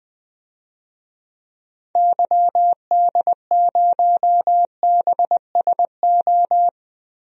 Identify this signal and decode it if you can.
Morse code, heard as a keyed tone, about 20 wpm, 700 Hz. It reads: YD0BSO
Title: DXCC-100 Call Signs - 6 Characters - Repeated 3x - 20wpm